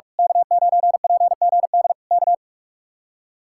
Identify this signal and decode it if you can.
Morse code, heard as a keyed tone, about 46 wpm, 700 Hz. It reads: K9PGD K
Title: Speed-Racing Common QSO Phrases - 45wpm